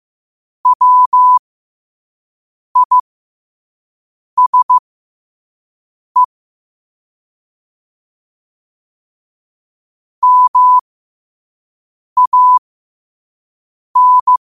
Morse code, heard as a keyed tone, about 15 wpm, spaced as for 5 wpm, 1000 Hz. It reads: WISE MAN
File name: qotdfiles / 5wpm